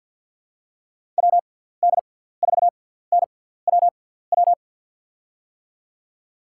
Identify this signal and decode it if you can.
Morse code, heard as a keyed tone, about 47 wpm, spaced as for 17 wpm, 700 Hz. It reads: WD4NWW
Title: USA Call Signs - 6 Characters - 6x Character Spacing - 50wpm